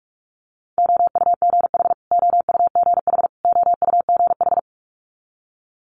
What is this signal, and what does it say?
Morse code, heard as a keyed tone, about 46 wpm, 700 Hz. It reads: OUGH OUGH OUGH
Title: Multi-letter Phonemes Repeated 3x 45wpm